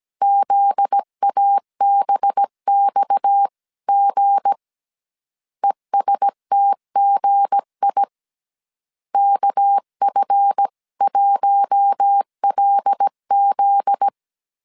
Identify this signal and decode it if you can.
Morse code, heard as a keyed tone, about 17 wpm, 800 Hz. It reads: ZABXG ESTGI KF1LZ
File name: CW / DET / 1 9 1500C S C 16wpm 2909